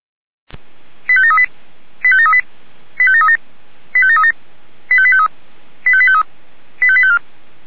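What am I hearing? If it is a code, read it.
Five-tone selective call (ZVEI), heard as quick runs of five tones, tones of about 80 ms. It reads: 85738 85738 85737 85737 85853 85853 85854
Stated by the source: ZVEI five-tone selective call 85825 received with an SDR.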